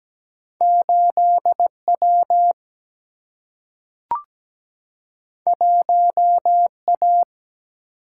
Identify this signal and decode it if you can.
Morse code, heard as a keyed tone, about 17 wpm, 700 Hz. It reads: 8W 1A
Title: Single-Letter Single-Number 17wpm